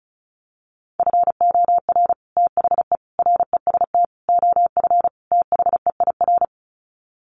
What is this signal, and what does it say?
Morse code, heard as a keyed tone, about 34 wpm, 700 Hz. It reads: FOR THE REST OF THEIR
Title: Sets of 5 Words Encore2 35wpm